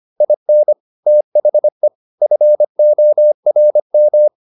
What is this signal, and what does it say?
Morse code, heard as a keyed tone, about 25 wpm, 600 Hz. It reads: IN THE FORM